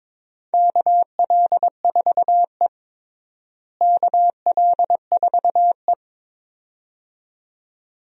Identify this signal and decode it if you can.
Morse code, heard as a keyed tone, about 22 wpm, 700 Hz. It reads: KL4E KL4E
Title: USA Call Signs - 4 Characters - Repeated 2x - 22wpm